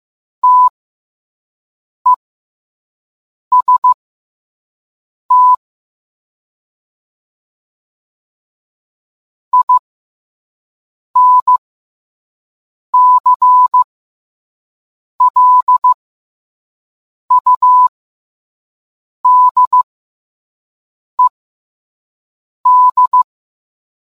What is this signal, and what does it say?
Morse code, heard as a keyed tone, about 15 wpm, spaced as for 5 wpm, 1000 Hz. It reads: TEST INCLUDED